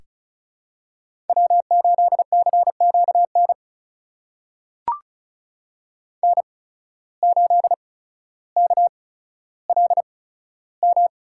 Morse code, heard as a keyed tone, about 34 wpm, spaced as for 15 wpm, 700 Hz. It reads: W8CQN N8KLM
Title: USA Call Signs - 5 Characters - 8x Character Spacing - 35wpm